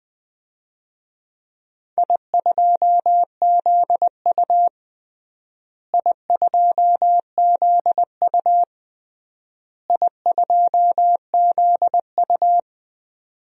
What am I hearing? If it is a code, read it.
Morse code, heard as a keyed tone, about 21 wpm, 700 Hz. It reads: I2ZU I2ZU I2ZU
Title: DXCC-100 Call Signs - 4 Characters - Repeated 3x - 20wpm